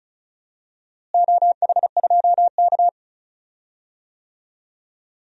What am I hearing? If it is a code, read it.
Morse code, heard as a keyed tone, about 36 wpm, 700 Hz. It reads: OH2K
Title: DXCC-100 Call Signs - 4 Characters - Repeated 2x - 35wpm